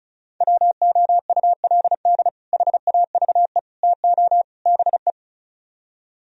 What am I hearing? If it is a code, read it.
Morse code, heard as a keyed tone, about 45 wpm, 700 Hz. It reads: WOULD HAVE TO BE